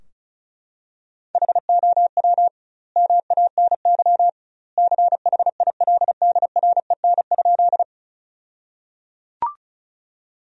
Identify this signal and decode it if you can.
Morse code, heard as a keyed tone, about 36 wpm, 700 Hz. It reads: HOW MANY CHILDREN?
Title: 2x Word-Spacing - Sentences from Top 300 Words - 35wpm